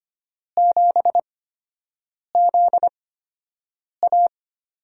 Morse code, heard as a keyed tone, about 25 wpm, spaced as for 7 wpm, 700 Hz. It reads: 7ZA
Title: USA Call Signs - 6 Characters - 8x Character Spacing - 25wpm